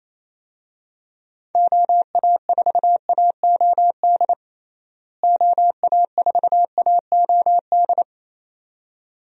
Morse code, heard as a keyed tone, about 28 wpm, 700 Hz. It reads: OA4AOD OA4AOD
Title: USA Call Signs - 6 Characters - Repeated 2x - 28wpm